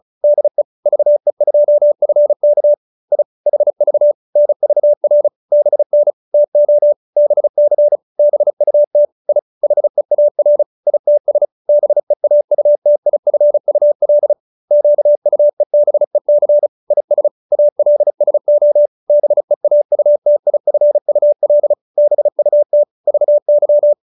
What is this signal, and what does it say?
Morse code, heard as a keyed tone, about 34 wpm, 590 Hz. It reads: DE VE2FK I HV NVR BN TO BC BUT I HEAR ITS BEAUTIFUL QUEBEC IS ALSO BEAUTIFUL BUT VY